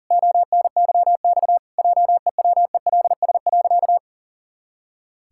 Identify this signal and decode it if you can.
Morse code, heard as a keyed tone, about 40 wpm, 700 Hz. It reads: ONYX JEWELS.
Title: Pangram Challenge v2 - 40wpm